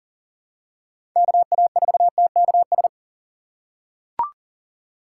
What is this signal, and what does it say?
Morse code, heard as a keyed tone, about 40 wpm, 700 Hz. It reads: KA4TKS
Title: USA Call Signs - 3x Character Spacing - 40wpm